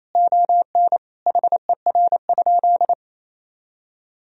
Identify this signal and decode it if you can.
Morse code, heard as a keyed tone, about 28 wpm, 700 Hz. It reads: ON HER?
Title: Sentences from Top 1000 Words - Review Part 4 - 28wpm